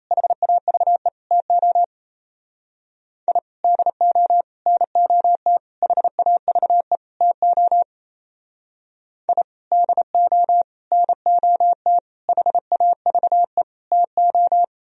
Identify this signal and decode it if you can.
Morse code, heard as a keyed tone, about 34 wpm, 700 Hz. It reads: HAVE TO I DO NOT HAVE TO I DO NOT HAVE TO
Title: Speed-Racing - Sets of 5 Words 28wpm